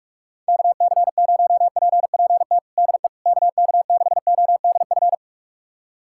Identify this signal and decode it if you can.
Morse code, heard as a keyed tone, about 42 wpm, 700 Hz. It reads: KC0PPT DE KK6ODF
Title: USA Callsign DE Callsign 45wpm